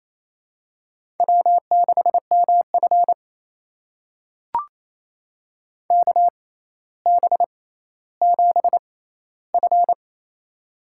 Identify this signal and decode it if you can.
Morse code, heard as a keyed tone, about 28 wpm, spaced as for 14 wpm, 700 Hz. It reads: W6MF KB7F
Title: USA Call Signs - 6x Character Spacing - 28wpm